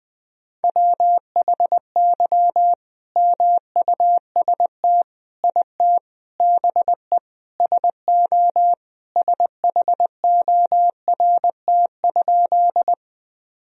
Morse code, heard as a keyed tone, about 21 wpm, 700 Hz. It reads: WHY MUST IT BE SO SHORT?
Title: Sentences from Top 1000 Words - Review Part 2 - 20wpm